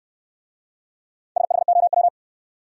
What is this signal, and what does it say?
Morse code, heard as a keyed tone, about 65 wpm, 700 Hz. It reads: SHOW